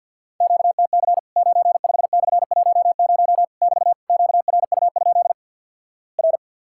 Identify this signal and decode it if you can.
Morse code, heard as a keyed tone, about 48 wpm, 700 Hz. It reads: QTC 95/10 <BT> QRU? R
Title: Worked All Europe DX Contest - QTC Exchange - 50wpm